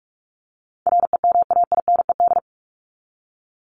Morse code, heard as a keyed tone, about 44 wpm, 700 Hz. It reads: REMAINED
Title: ED Suffix - Top 100 Words with Base Words 45wpm